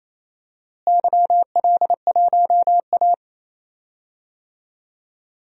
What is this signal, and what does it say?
Morse code, heard as a keyed tone, about 28 wpm, 700 Hz. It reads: YL1A
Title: DXCC-100 Call Signs - 4 Characters - Repeated 2x - 28wpm